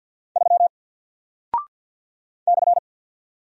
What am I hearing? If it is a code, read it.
Morse code, heard as a keyed tone, about 50 wpm, 700 Hz. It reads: 3 /
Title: Single Character (Letter:Number:Punctuation:Prosign) 50wpm